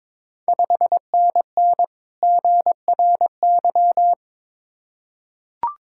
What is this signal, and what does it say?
Morse code, heard as a keyed tone, about 22 wpm, 690 Hz. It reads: 5NN GRY
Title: Ontario QSO Party Exchange 22wpm